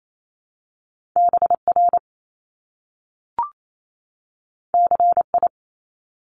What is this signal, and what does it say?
Morse code, heard as a keyed tone, about 28 wpm, 700 Hz. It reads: BR CI
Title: Top Two Letter Combinations in Words 28wpm